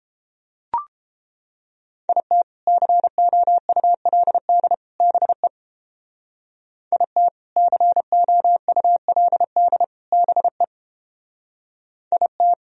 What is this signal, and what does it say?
Morse code, heard as a keyed tone, about 29 wpm, 700 Hz. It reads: IT COULD BE IT COULD BE IT